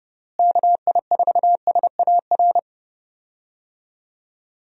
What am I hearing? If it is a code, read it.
Morse code, heard as a keyed tone, about 30 wpm, 700 Hz. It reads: KI4SAR
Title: USA Call Signs - 6 Characters - Repeated 3x - 30wpm